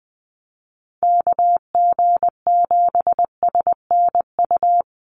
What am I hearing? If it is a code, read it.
Morse code, heard as a keyed tone, about 20 wpm, 700 Hz. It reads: KG7SNU
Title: USA Call Signs 20wpm